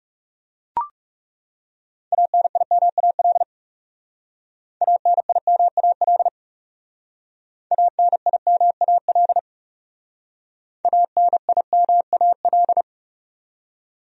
Morse code, heard as a keyed tone, about 39 wpm, 700 Hz. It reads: ANIMAL ANIMAL ANIMAL ANIMAL